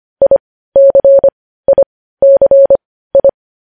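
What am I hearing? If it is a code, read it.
Morse code, heard as a keyed tone, about 25 wpm, spaced as for 15 wpm, 550 Hz. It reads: ICICI
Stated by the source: Morse code news headlines at 15 WPM.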